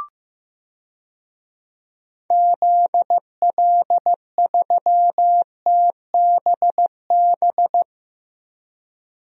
Morse code, heard as a keyed tone, about 15 wpm, 700 Hz. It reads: ZL3TBB
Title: DXCC-100 Call Signs - 6 Characters - 15wpm